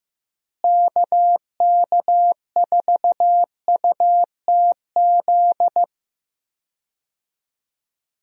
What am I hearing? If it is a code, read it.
Morse code, heard as a keyed tone, about 15 wpm, 700 Hz. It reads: KK4UTZ
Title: USA Call Signs - 6 Characters - Repeated 2x - 15wpm